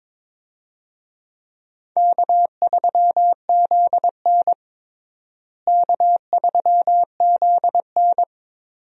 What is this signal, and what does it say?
Morse code, heard as a keyed tone, about 21 wpm, 700 Hz. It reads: K3ZN K3ZN
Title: USA Call Signs - 4 Characters - Repeated 2x - 22wpm